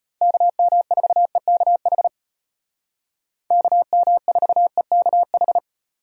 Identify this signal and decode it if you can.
Morse code, heard as a keyed tone, about 32 wpm, 700 Hz. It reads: KM4EKH KM4EKH